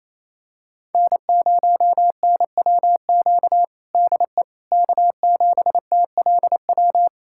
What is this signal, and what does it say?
Morse code, heard as a keyed tone, about 28 wpm, 700 Hz. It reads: N0NWQ DE K7TLW
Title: USA Callsign DE Callsign - 5 Characters - 28wpm